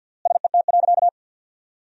Morse code, heard as a keyed tone, about 52 wpm, 700 Hz. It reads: SET.
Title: 3x Word-Spacing - Sentences from Top 400 Words - 50wpm